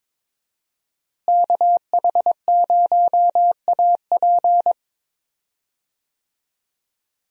Morse code, heard as a keyed tone, about 22 wpm, 700 Hz. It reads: KH0AP
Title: USA Call Signs - 5 Characters - Repeated 3x - 22wpm